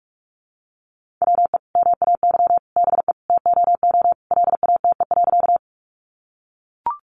Morse code, heard as a keyed tone, about 44 wpm, 700 Hz. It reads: WE MAY BE TOO LATE.
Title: Sentences from Top 300 Words 45wpm